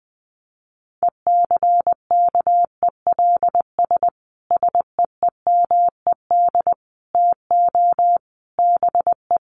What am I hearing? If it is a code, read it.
Morse code, heard as a keyed tone, about 20 wpm, 700 Hz. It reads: ECKELS SEEMED TO BE
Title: A Sound Of Thunder 20wpm